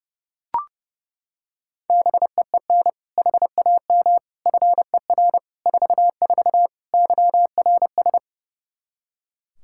Morse code, heard as a keyed tone, about 30 wpm, 700 Hz. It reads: BEEN HAM FER 44 YRS
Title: Practice QSO - Exchange 3 - More Info - 30wpm